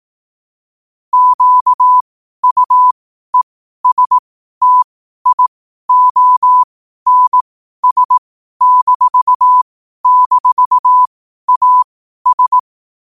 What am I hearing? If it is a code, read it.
Morse code, heard as a keyed tone, about 18 wpm, spaced as for 12 wpm, 1000 Hz. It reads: QUESTIONS--AS
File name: qotdfiles / 12wpm